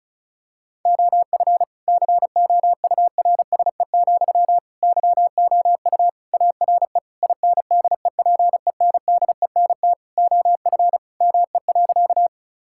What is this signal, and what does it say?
Morse code, heard as a keyed tone, about 36 wpm, 700 Hz. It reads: OF COURSE, YOU ARE INDEPENDENT OF ME.
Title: Sentences from Top 1000 Words - Review Part 6 - 35wpm